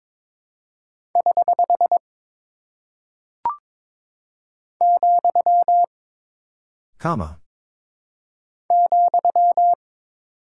Morse code, heard as a keyed tone, about 22 wpm, 700 Hz. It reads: <HH> , ,
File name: practice / 22wpm / Single Punctuation-Prosign - 22wpm